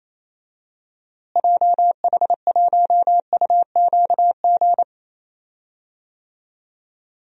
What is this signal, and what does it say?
Morse code, heard as a keyed tone, about 27 wpm, 700 Hz. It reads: JH1UQG
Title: DXCC-100 Call Signs - 6 Characters - Repeated 3x - 28wpm